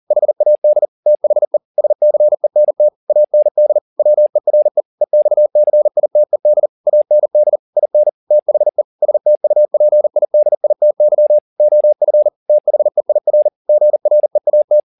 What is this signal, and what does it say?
Morse code, heard as a keyed tone, about 40 wpm, 600 Hz. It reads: HAD THE SCENT AND WERE EXCITED AND IN THE STUPIDITY OF THEIR GREAT
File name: the old man and the sea 2 40wpm